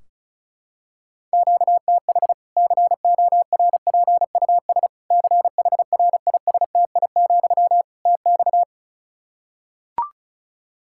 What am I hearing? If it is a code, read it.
Morse code, heard as a keyed tone, about 36 wpm, 700 Hz. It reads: QTH CORPUS CHRISTI, TX
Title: Practice QSO - Exchange 1 - Beginning - 35wpm